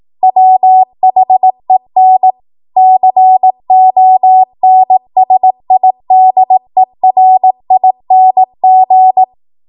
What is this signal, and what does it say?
Morse code, heard as a keyed tone, about 18 wpm, 750 Hz. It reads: WHEN CONSIDERING
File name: Morse / 18WPM